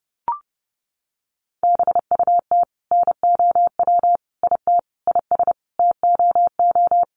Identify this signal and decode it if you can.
Morse code, heard as a keyed tone, about 30 wpm, 700 Hz. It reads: BUT NOW IT IS TOO
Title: Sentences from Top 300 Words 30wpm